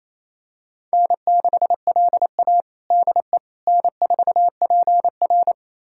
Morse code, heard as a keyed tone, about 28 wpm, 700 Hz. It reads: N6LA DE N4PR
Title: USA Callsign DE Callsign - 4 Characters - 28wpm